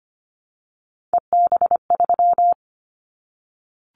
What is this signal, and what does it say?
Morse code, heard as a keyed tone, about 25 wpm, 700 Hz. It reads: EB3
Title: Callsign Prefixes for Top 25 DXCC - Repeated 3x - 25wpm